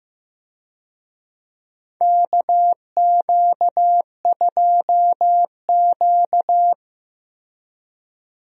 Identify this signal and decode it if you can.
Morse code, heard as a keyed tone, about 15 wpm, 700 Hz. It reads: KQ2Q
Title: USA Call Signs - 4 Characters - Repeated 2x - 15wpm